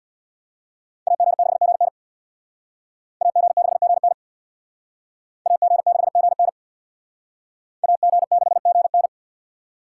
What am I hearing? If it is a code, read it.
Morse code, heard as a keyed tone, about 68 wpm, 700 Hz. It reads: AG6GN AG6GN AG6GN AG6GN